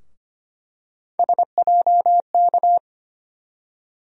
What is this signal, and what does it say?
Morse code, heard as a keyed tone, about 25 wpm, 700 Hz. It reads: SJK
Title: USA Call Sign Postfixes - 3 Letters 25wpm